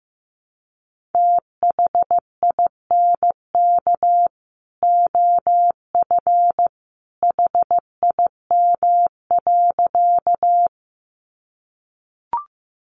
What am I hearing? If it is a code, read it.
Morse code, heard as a keyed tone, about 15 wpm, 700 Hz. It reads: THINK OF HIM.
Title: Sentences from Top 200 Words 15wpm